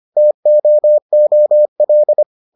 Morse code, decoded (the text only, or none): TOOL